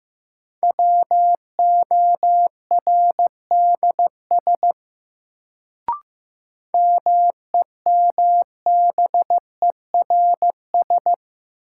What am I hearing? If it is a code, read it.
Morse code, heard as a keyed tone, about 15 wpm, 700 Hz. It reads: WORDS MEMBERS
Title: Sets of 3 Words - New Word Drill - 15wpm